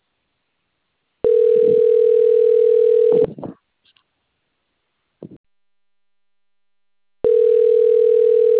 Ringback tone (North America), 440 + 480 Hz.